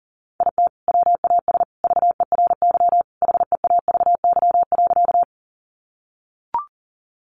Morse code, heard as a keyed tone, about 40 wpm, 700 Hz. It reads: IT WAS VERY HEAVY.
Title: Sentences from Top 1100 Words 40wpm